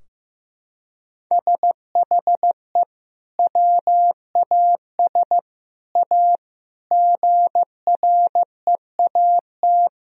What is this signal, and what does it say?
Morse code, heard as a keyed tone, about 15 wpm, 700 Hz. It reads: SHE WAS A GREAT
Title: Sentences from Top 1000 Words - Review Part 4 - 15wpm